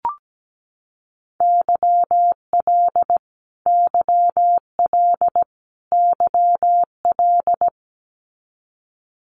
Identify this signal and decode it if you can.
Morse code, heard as a keyed tone, about 17 wpm, 700 Hz. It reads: YL YL YL
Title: Abbreviations Repeated 3x - 17wpm